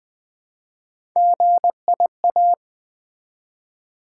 Morse code, heard as a keyed tone, about 20 wpm, 700 Hz. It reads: GIA
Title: Build A Word - 1 Letter At A Time - 3 to 5-Letter Words - 20wpm